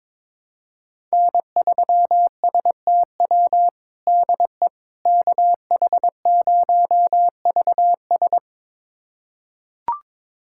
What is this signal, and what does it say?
Morse code, heard as a keyed tone, about 22 wpm, 700 Hz. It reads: N3STW DE KH0VS
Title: USA Callsign DE Callsign - 5 Characters - 22wpm